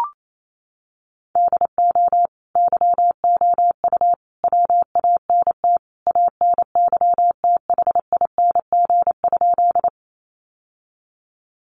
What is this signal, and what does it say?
Morse code, heard as a keyed tone, about 28 wpm, 700 Hz. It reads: DO YOU WANT ANYTHING?
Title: Sentences from Top 1000 Words 28wpm